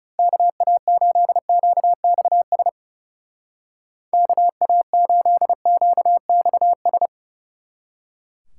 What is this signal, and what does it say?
Morse code, heard as a keyed tone, about 35 wpm, 700 Hz. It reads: KA8QXS KA8QXS
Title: Speed Racing - USA 2x3 Call Signs 30wpm